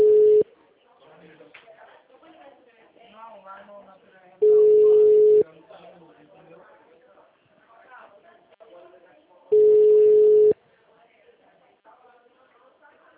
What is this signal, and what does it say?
Ringback tone (Europe, 425 Hz), with speech.
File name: out- -2015- - - .94249